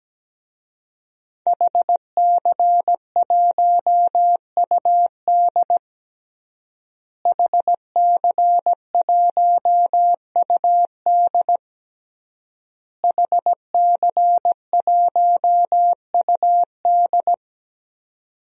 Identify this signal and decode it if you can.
Morse code, heard as a keyed tone, about 17 wpm, 700 Hz. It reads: HC1UD HC1UD HC1UD
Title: DXCC-100 Call Signs - 5 Characters - Repeated 3x - 17wpm